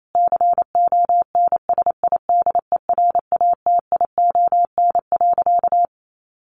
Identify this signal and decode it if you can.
Morse code, heard as a keyed tone, about 28 wpm, 700 Hz. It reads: CONSIDERATION.
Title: Sentences from Top 1500 Words 28wpm